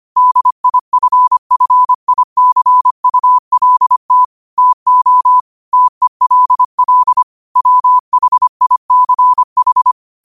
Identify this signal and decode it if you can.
Morse code, heard as a keyed tone, about 25 wpm, 1000 Hz. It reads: DIFFICULT TO TELL WHICH
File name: qotdfiles / 25wpm